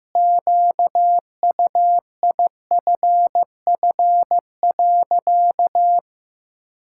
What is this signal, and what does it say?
Morse code, heard as a keyed tone, about 15 wpm, 700 Hz. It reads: QUIFF.
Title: Pangram Challenge v2 - 15wpm